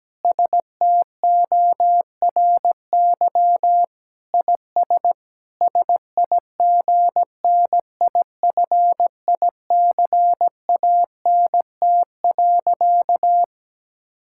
Morse code, heard as a keyed tone, about 17 wpm, 700 Hz. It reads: STORY IS SIGNIFICANT.
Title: Sentences from Top 1000 Words - Review Part 6 - 17wpm